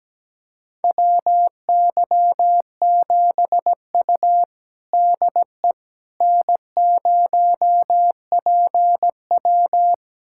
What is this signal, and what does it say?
Morse code, heard as a keyed tone, about 17 wpm, 700 Hz. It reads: WY7U DE N0PW
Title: USA Callsign DE Callsign - 4 Characters - 17wpm